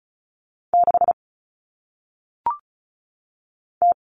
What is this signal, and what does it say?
Morse code, heard as a keyed tone, about 35 wpm, 700 Hz. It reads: 6 T
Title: Common Cut Numbers - 1-Digit 35wpm